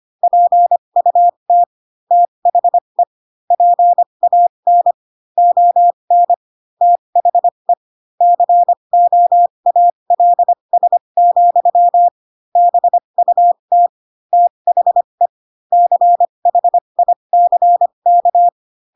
Morse code, heard as a keyed tone, about 25 wpm, spaced as for 22 wpm, 700 Hz. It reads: PUT THE PAN ON THE COALS, BUT THE CHICK
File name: Pinocchio / Pinocchio ch7 18 25 ik1hge